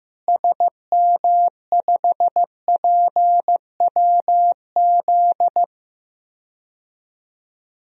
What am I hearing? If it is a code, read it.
Morse code, heard as a keyed tone, about 16 wpm, 700 Hz. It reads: SM5PWZ
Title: DXCC-100 Call Signs - 6 Characters - Repeated 3x - 15wpm